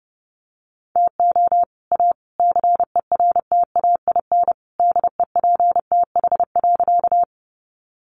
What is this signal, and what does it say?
Morse code, heard as a keyed tone, about 30 wpm, 700 Hz. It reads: TO A CERTAIN DEPTH.
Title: Sentences from Top 1200 Words 30wpm